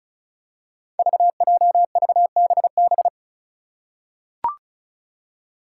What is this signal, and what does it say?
Morse code, heard as a keyed tone, about 34 wpm, 700 Hz. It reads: VJVBB
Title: Five Challenging Letter-Number Combinations 35wpm